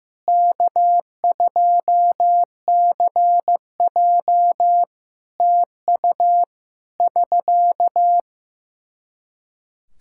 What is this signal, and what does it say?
Morse code, heard as a keyed tone, about 15 wpm, 700 Hz. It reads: K2CJ TU <SK>
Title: K1USN SST QSO Practice 15wpm